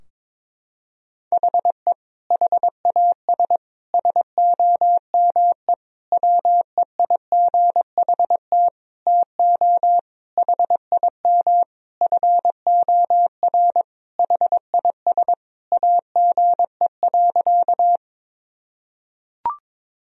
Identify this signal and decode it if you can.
Morse code, heard as a keyed tone, about 22 wpm, 700 Hz. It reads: HE HAS SOME WEIGHT TO HIM FOR HIS AGE.
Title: Sentences from Top 1000 Words - Review Part 3 - 22wpm